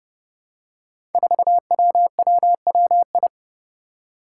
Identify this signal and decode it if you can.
Morse code, heard as a keyed tone, about 30 wpm, 700 Hz. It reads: 4WWWI
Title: Five Challenging Letter-Number Combinations 30wpm